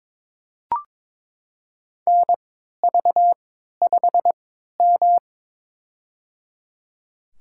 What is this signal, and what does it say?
Morse code, heard as a keyed tone, about 22 wpm, spaced as for 12 wpm, 700 Hz. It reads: NV5M
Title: USA Call Signs - 4 Characters - 3x Character Spacing - 22wpm